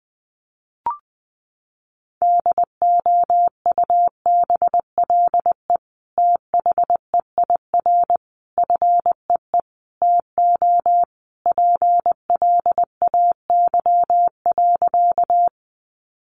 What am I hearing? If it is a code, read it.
Morse code, heard as a keyed tone, about 20 wpm, 700 Hz. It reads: DOUBLE THEIR FEE TO PLAY.
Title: Sentences from Top 1500 Words 20wpm